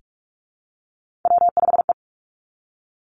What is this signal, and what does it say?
Morse code, heard as a keyed tone, about 46 wpm, 700 Hz. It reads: W5E
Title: Three Challenging Letter-Number Combinations 45wpm